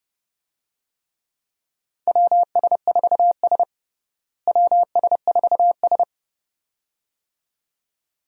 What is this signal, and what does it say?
Morse code, heard as a keyed tone, about 30 wpm, 700 Hz. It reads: WS4S WS4S
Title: USA Call Signs - 4 Characters - Repeated 2x - 30wpm